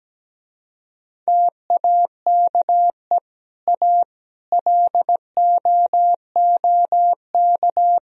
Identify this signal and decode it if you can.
Morse code, heard as a keyed tone, about 17 wpm, 700 Hz. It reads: TAKE A LOOK
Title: Speed-Racing - Sets of 5 Words 17wpm